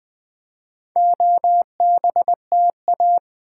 Morse code, heard as a keyed tone, about 20 wpm, 700 Hz. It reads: OBTA
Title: Build A Word - 2 Letters At A Time - 6-Letter Words - 20wpm